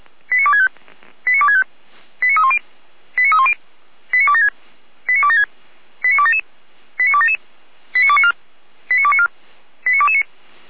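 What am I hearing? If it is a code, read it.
Five-tone selective call (ZVEI), heard as quick runs of five tones, tones of about 70 ms. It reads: 89265 89265 89310 89310 89276 89276 89277 89277 89294 89294 89229
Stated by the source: ZVEI five-tone call 89174 received on an SDR.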